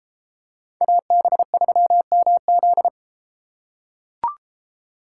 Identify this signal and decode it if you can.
Morse code, heard as a keyed tone, about 29 wpm, 700 Hz. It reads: AB3MZ